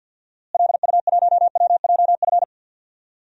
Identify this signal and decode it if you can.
Morse code, heard as a keyed tone, about 52 wpm, 700 Hz. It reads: LU1WJF
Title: DXCC-100 Call Signs - 6 Characters - Repeated 2x - 50wpm